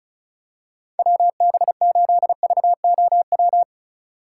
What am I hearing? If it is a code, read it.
Morse code, heard as a keyed tone, about 36 wpm, 700 Hz. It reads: WB8VOW
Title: USA Call Signs - 6 Characters - 3x Character Spacing - 35wpm